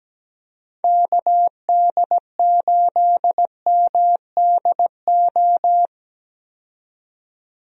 Morse code, heard as a keyed tone, about 17 wpm, 700 Hz. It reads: KD8MDO
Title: USA Call Signs - 6 Characters - 17wpm